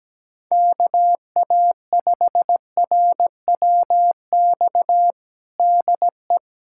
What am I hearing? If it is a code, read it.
Morse code, heard as a keyed tone, about 17 wpm, 700 Hz. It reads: KA5RWX DE
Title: USA Callsign DE Callsign - 6 Characters - 17wpm